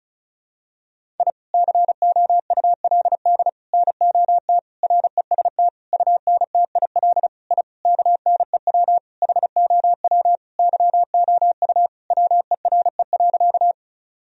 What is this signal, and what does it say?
Morse code, heard as a keyed tone, about 36 wpm, 700 Hz. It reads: I COULD NOT REST UNTIL I KNEW HOW YOU WERE.
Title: Sentences from Top 1000 Words - Review Part 1 - 35wpm